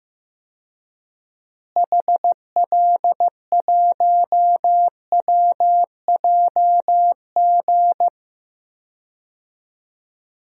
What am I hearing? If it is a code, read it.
Morse code, heard as a keyed tone, about 15 wpm, 700 Hz. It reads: HL1WJG
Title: DXCC-100 Call Signs - 6 Characters - Repeated 2x - 15wpm